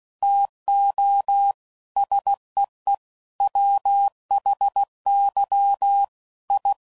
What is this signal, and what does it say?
Morse code, heard as a keyed tone, about 16 wpm, 800 Hz. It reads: TO SEE WHY I
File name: CW Code Groups @15WPM / Technical Subjects / OR Gate